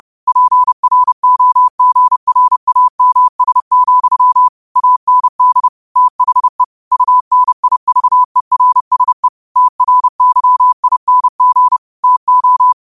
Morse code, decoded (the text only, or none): PROGRAMS, AND THE UNIVERSE TRYING TO